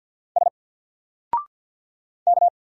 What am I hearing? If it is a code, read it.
Morse code, heard as a keyed tone, about 50 wpm, 700 Hz. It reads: S K
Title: Single Character (Letter:Number:Punctuation:Prosign) 50wpm